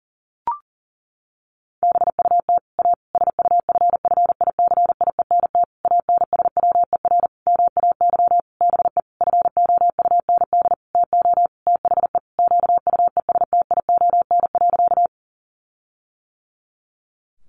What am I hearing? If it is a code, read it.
Morse code, heard as a keyed tone, about 37 wpm, 700 Hz. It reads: BUT A SUFFICIENT ANSWER MAY BE FOUND TO THE QUESTION.
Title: Sentences from Top 1400 Words 40wpm